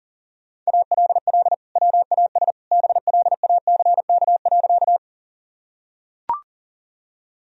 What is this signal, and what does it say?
Morse code, heard as a keyed tone, about 40 wpm, 700 Hz. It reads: ALL WAS BLACK.
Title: Sentences from Top 1000 Words - Review Part 2 - 40wpm